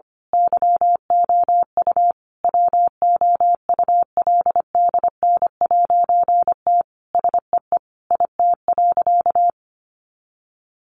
Morse code, heard as a keyed tone, about 25 wpm, 700 Hz. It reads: YOU WOULDN'T SEE IT.
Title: A Sound Of Thunder 25wpm